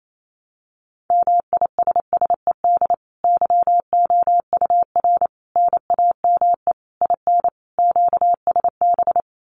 Morse code, heard as a keyed tone, about 28 wpm, 700 Hz. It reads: MISSED YOUR NAME IN QSB
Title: Common QSO Phrases 28wpm